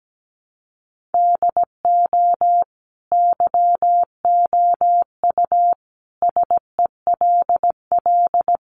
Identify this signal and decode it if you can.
Morse code, heard as a keyed tone, about 17 wpm, 700 Hz. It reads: DO YOU SELL
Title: Sentences from Top 1400 Words 17wpm